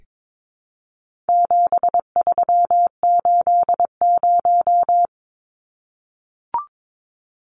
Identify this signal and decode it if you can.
Morse code, heard as a keyed tone, about 22 wpm, 700 Hz. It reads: 7380
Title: Numbers - 4-Digit 22wpm